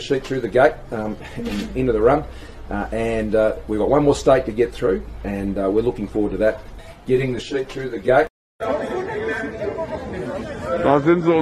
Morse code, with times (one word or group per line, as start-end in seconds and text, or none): none